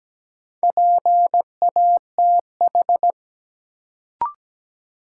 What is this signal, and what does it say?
Morse code, heard as a keyed tone, about 17 wpm, 700 Hz. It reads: PATH
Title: Sentences from Top 1400 Words - New Word Drill - 17wpm